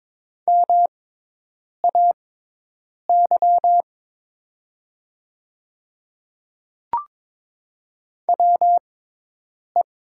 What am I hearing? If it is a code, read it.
Morse code, heard as a keyed tone, about 22 wpm, spaced as for 8 wpm, 700 Hz. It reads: MAY WE
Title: 6x Character Spacing - Top 100 Words - 22wpm